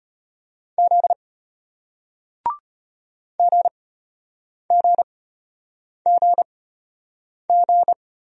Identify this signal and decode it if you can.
Morse code, heard as a keyed tone, about 33 wpm, spaced as for 8 wpm, 700 Hz. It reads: Z GGGG